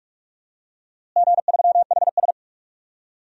Morse code, heard as a keyed tone, about 46 wpm, 700 Hz. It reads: G3HS
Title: DXCC-100 Call Signs - 4 Characters - Repeated 3x - 45wpm